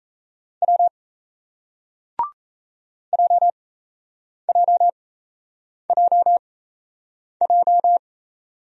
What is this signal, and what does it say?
Morse code, heard as a keyed tone, about 38 wpm, spaced as for 8 wpm, 700 Hz. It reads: W JJJJ